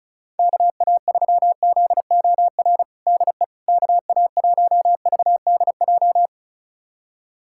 Morse code, heard as a keyed tone, about 34 wpm, 700 Hz. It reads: KA3ZOR DE KA1VDJ
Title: USA Callsign DE Callsign - 6 Characters - 35wpm